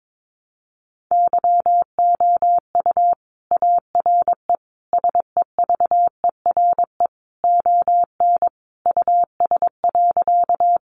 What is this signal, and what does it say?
Morse code, heard as a keyed tone, about 22 wpm, 700 Hz. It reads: YOU ARE SEVERE ON US.
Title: Sentences from Top 1400 Words 22wpm